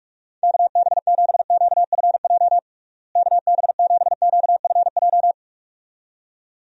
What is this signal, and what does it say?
Morse code, heard as a keyed tone, about 46 wpm, 700 Hz. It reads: KB7QFJ KB7QFJ
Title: USA Call Signs - 6 Characters - Repeated 2x - 45wpm